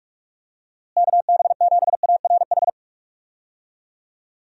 Morse code, heard as a keyed tone, about 46 wpm, 700 Hz. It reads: KB7ARH
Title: USA Call Signs - 6 Characters - Repeated 3x - 45wpm